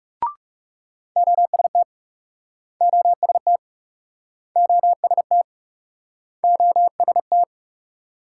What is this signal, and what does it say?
Morse code, heard as a keyed tone, about 41 wpm, 700 Hz. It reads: OST OST OST OST